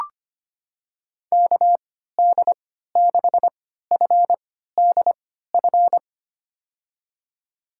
Morse code, heard as a keyed tone, about 25 wpm, spaced as for 14 wpm, 700 Hz. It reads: KD6FDF
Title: USA Call Signs - 6 Characters - 3x Character Spacing - 25wpm